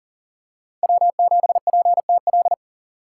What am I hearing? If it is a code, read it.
Morse code, heard as a keyed tone, about 40 wpm, 700 Hz. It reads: W7PTL
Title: USA Call Signs - 5 Characters - 6x Character Spacing - 40wpm